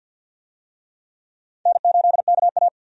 Morse code, heard as a keyed tone, about 46 wpm, 700 Hz. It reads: N8KA
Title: USA Call Signs - 4 Characters - 50wpm